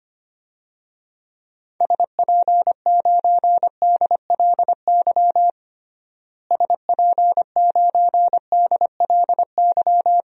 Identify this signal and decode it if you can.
Morse code, heard as a keyed tone, about 25 wpm, 700 Hz. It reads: SP9DLY SP9DLY
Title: DXCC-100 Call Signs - 6 Characters - Repeated 3x - 25wpm